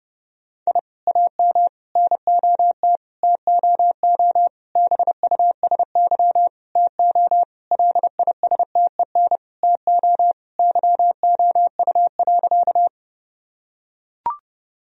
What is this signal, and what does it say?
Morse code, heard as a keyed tone, about 32 wpm, 700 Hz. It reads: I AM NOT TOO BUSY TO LISTEN TO YOU.
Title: Sentences from Top 1000 Words - Review Part 3 - 30wpm